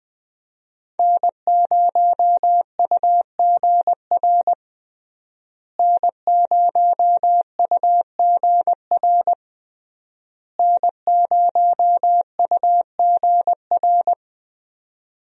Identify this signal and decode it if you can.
Morse code, heard as a keyed tone, about 21 wpm, 700 Hz. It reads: N0UGR N0UGR N0UGR
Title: USA Call Signs - 5 Characters - Repeated 3x - 20wpm